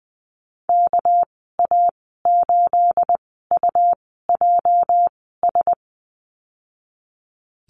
Morse code, heard as a keyed tone, about 20 wpm, spaced as for 15 wpm, 700 Hz. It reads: KA8UJS
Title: 213 Lesson 34g - Number 8 - Call Signs 20wpm